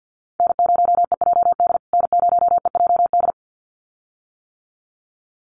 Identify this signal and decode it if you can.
Morse code, heard as a keyed tone, about 54 wpm, 700 Hz. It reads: N0EJD N0EJD
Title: USA Call Signs Repeated 2x 50wpm